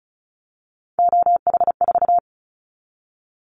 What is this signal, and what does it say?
Morse code, heard as a keyed tone, about 34 wpm, 700 Hz. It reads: OH4
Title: Callsign Prefixes for Top 25 DXCC - Repeated 3x - 35wpm